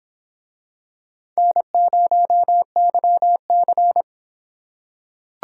Morse code, heard as a keyed tone, about 26 wpm, 700 Hz. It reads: N0YC